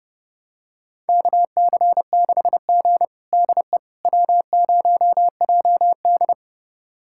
Morse code, heard as a keyed tone, about 30 wpm, 700 Hz. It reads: KC6G DE W0JD
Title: USA Callsign DE Callsign - 4 Characters - 30wpm